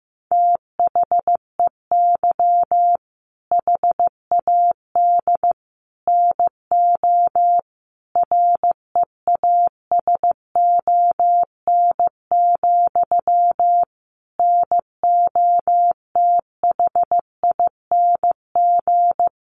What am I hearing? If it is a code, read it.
Morse code, heard as a keyed tone, about 15 wpm, 700 Hz. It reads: THEY HAD NO REASON, NOTHING
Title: Sentences from Top 1500 Words 15wpm